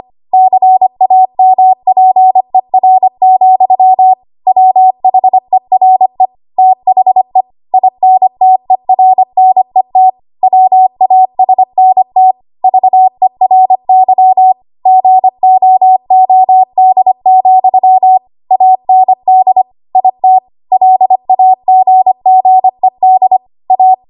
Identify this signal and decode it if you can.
Morse code, heard as a keyed tone, about 25 wpm, 750 Hz. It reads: CAMPER, WHERE THE INTERNET WASNT VERY GOOD, AND IT LAGGED A
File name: Archive / 25 WPM / 25WPM